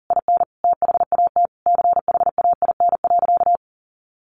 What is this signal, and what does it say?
Morse code, heard as a keyed tone, about 40 wpm, 700 Hz. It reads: IN THAT CHAIN.
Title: Sentences from Top 1500 Words 40wpm